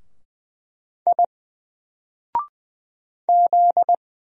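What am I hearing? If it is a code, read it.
Morse code, heard as a keyed tone, about 20 wpm, 700 Hz. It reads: I Z
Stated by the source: Single letter and number Morse drill at 20 wpm, from N4LCD.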